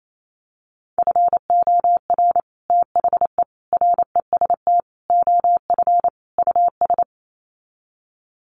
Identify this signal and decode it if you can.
Morse code, heard as a keyed tone, about 28 wpm, 700 Hz. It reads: FOR THE REST OF US
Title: Sets of 5 Words Encore 28wpm